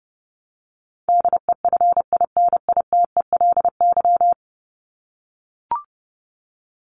Morse code, heard as a keyed tone, about 30 wpm, 700 Hz. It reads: DEFINITELY
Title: Top 1751-2000 Words 30wpm